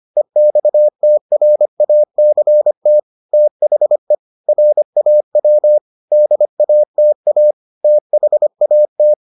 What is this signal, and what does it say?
Morse code, heard as a keyed tone, about 25 wpm, 600 Hz. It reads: EXTRACT THE RAW DATA THAT